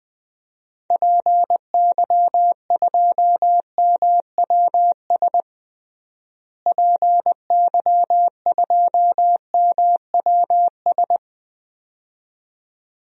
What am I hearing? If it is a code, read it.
Morse code, heard as a keyed tone, about 20 wpm, 700 Hz. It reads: PY2MWS PY2MWS
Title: DXCC-100 Call Signs - 6 Characters - Repeated 2x - 20wpm